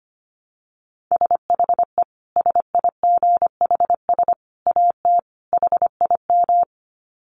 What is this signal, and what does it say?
Morse code, heard as a keyed tone, about 25 wpm, 700 Hz. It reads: SHE SIGHS AT HIM
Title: Dit Challenge 25wpm